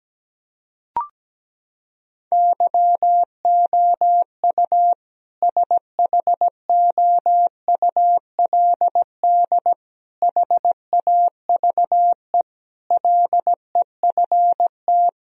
Morse code, heard as a keyed tone, about 17 wpm, 700 Hz. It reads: YOU SHOULD HAVE LEFT
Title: Sentences from Top 1000 Words - Review Part 1 - 17wpm